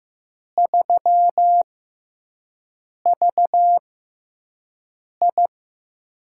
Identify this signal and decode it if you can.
Morse code, heard as a keyed tone, about 15 wpm, spaced as for 5 wpm, 700 Hz. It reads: 3VI
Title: USA Call Signs - 4 Characters - 6x Character Spacing - 15wpm